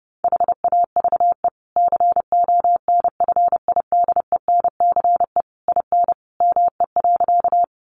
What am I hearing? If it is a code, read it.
Morse code, heard as a keyed tone, about 30 wpm, 700 Hz. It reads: HAVE CONFIDENCE IN ME.
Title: Sentences from Top 1500 Words 30wpm